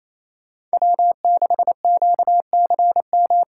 Morse code, heard as a keyed tone, about 28 wpm, 700 Hz. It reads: W6QCM
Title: USA Call Signs - 5 Characters - 2x Character Spacing - 28wpm